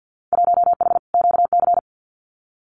Morse code, heard as a keyed tone, about 50 wpm, 700 Hz. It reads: 1H QC
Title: Winter Field Day Exchanges 50wpm